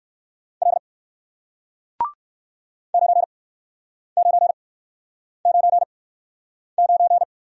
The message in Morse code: L 9999

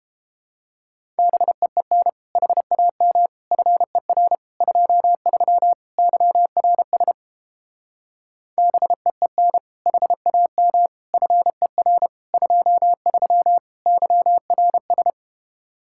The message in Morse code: BEEN HAM FER 23 YRS BEEN HAM FER 23 YRS

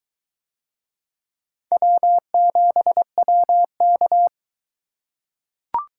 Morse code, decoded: W7WK

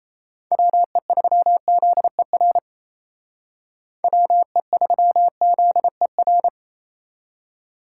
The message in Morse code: WE3ZER WE3ZER

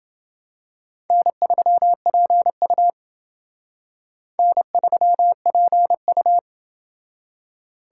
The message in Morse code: N3PU N3PU

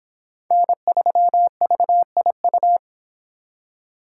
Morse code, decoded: N3VIU

